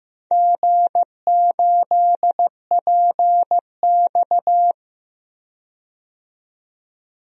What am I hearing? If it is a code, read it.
Morse code, heard as a keyed tone, about 15 wpm, 700 Hz. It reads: G8PX
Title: DXCC-100 Call Signs - 4 Characters - 15wpm